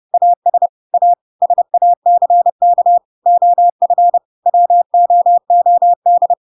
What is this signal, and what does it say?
Morse code, heard as a keyed tone, about 30 wpm, 700 Hz. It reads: AS A SACK OF WOOD
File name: Pinocchio / Pinocchio ch7 30 30 ik1hge